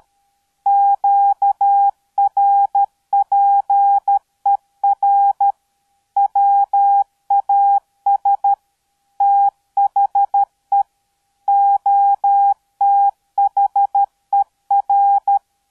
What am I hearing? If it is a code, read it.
Morse code, heard as a keyed tone, about 13 wpm, 800 Hz. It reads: QRPER WAS THE OTHER